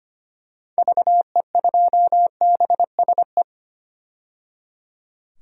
Morse code, heard as a keyed tone, about 25 wpm, 700 Hz. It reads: VE2BSE
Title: DXCC-100 Call Signs - 6 Characters - 25wpm